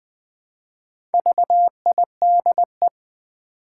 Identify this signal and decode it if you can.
Morse code, heard as a keyed tone, about 20 wpm, 700 Hz. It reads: VIDE
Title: Build A Word - 1 Letter At A Time - 5-Letter Words - 20wpm